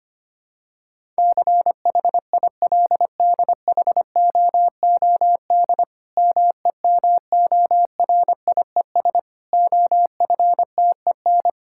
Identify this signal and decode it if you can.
Morse code, heard as a keyed tone, about 25 wpm, 700 Hz. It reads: CHILDHOOD MEMORIES OFTEN
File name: practice / 25wpm / Difficult Sentences with Hints - 25wpm